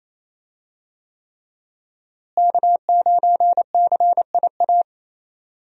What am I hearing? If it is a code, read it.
Morse code, heard as a keyed tone, about 28 wpm, 700 Hz. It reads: K9CIA
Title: USA Call Signs - 5 Characters - Repeated 3x - 28wpm